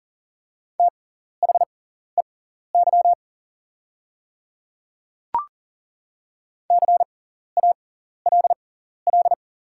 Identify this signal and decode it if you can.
Morse code, heard as a keyed tone, about 40 wpm, spaced as for 14 wpm, 700 Hz. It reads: THEY CALL
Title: 6x Character Spacing - Top 100 Words - 40wpm